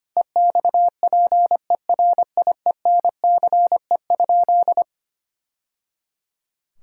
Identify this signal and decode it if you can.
Morse code, heard as a keyed tone, about 25 wpm, 700 Hz. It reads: EXPERIENCE?
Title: Sentences from Top 1000 Words - Review Part 5 - 25wpm